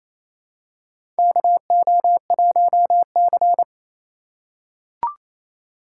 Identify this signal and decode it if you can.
Morse code, heard as a keyed tone, about 28 wpm, 700 Hz. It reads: KO1C
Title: USA Call Signs - 4x Character Spacing - 28wpm